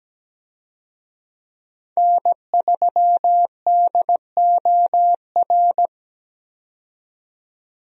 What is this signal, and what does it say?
Morse code, heard as a keyed tone, about 17 wpm, 700 Hz. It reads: N3DOR
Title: USA Call Signs - 5 Characters - 17wpm